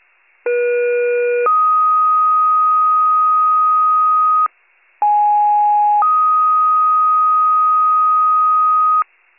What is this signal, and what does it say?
Two-tone page, heard as a one-second tone, then a three-second tone, about 480 then 1250 Hz, about 830 then 1250 Hz. It is heard from 0.5 to 9.0 s.